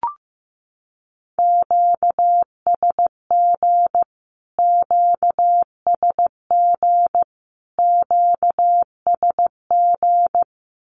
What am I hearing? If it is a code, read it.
Morse code, heard as a keyed tone, about 15 wpm, 700 Hz. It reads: QSG QSG QSG
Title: Q-Codes Repeated 3x 15wpm